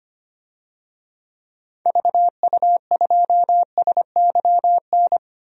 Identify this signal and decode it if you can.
Morse code, heard as a keyed tone, about 25 wpm, 700 Hz. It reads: VU2SYN
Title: DXCC-100 Call Signs - 6 Characters - 25wpm